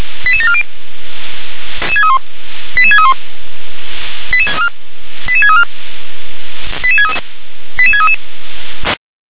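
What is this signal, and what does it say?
Five-tone selective call (ZVEI), heard as quick runs of five tones, tones of about 70 ms. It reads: 88740 88631 88634 88633 88633
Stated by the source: Five-tone ZVEI selective call received on an SDR.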